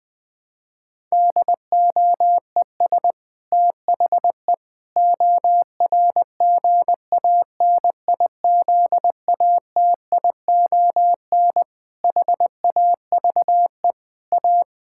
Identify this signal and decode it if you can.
Morse code, heard as a keyed tone, about 20 wpm, 700 Hz. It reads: DOES THE ORGANIZATION HAVE A
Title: Sentences from Top 1000 Words - Review Part 6 - 20wpm